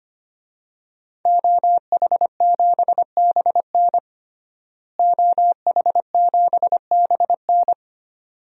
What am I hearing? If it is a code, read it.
Morse code, heard as a keyed tone, about 25 wpm, 700 Hz. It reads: OH7BN OH7BN
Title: DXCC-100 Call Signs - 5 Characters - Repeated 2x - 25wpm